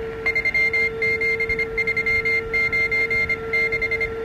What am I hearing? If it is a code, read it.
Morse code, heard as a keyed tone, about 25 wpm, 2100 Hz. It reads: 37396